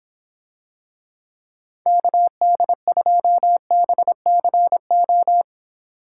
Morse code, heard as a keyed tone, about 26 wpm, 700 Hz. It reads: KD2BCO